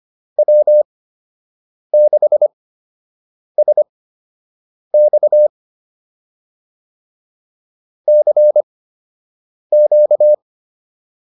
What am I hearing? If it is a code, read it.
Morse code, heard as a keyed tone, about 25 wpm, spaced as for 7 wpm, 600 Hz. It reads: W6SX CQ